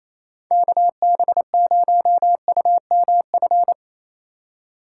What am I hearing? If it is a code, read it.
Morse code, heard as a keyed tone, about 28 wpm, 700 Hz. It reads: KB0UMF
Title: USA Call Signs - 6 Characters - 4x Character Spacing - 28wpm